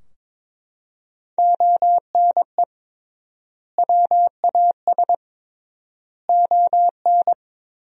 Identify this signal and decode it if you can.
Morse code, heard as a keyed tone, about 21 wpm, 700 Hz. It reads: ONE WAS ON
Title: 3x Word-Spacing - Sentences from Top 300 Words - 22wpm